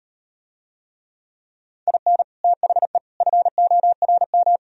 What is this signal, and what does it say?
Morse code, heard as a keyed tone, about 39 wpm, 700 Hz. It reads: IN THE FORM